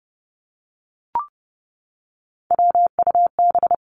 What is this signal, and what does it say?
Morse code, heard as a keyed tone, about 30 wpm, 700 Hz. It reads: WUB
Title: Three Challenging Letter-Number Combinations 30wpm